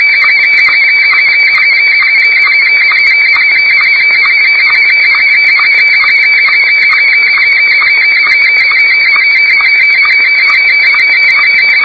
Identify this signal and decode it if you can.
Two-tone teleprinter signal, about 170 Hz shift.